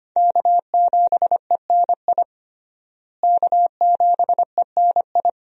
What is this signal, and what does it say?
Morse code, heard as a keyed tone, about 25 wpm, 700 Hz. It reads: K7ENI K7ENI
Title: USA Call Signs - 5 Characters - Repeated 2x - 25wpm